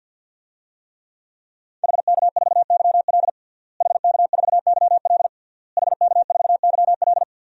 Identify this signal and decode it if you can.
Morse code, heard as a keyed tone, about 49 wpm, 700 Hz. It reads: HK4YL HK4YL HK4YL
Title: DXCC-100 Call Signs - 5 Characters - Repeated 3x - 50wpm